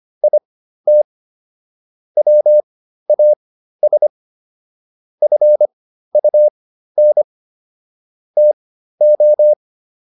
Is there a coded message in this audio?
Yes: Morse code from 0.2 to 9.5 s, heard as a keyed tone, about 25 wpm, spaced as for 13 wpm, 600 Hz. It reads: IT WAS FUN TO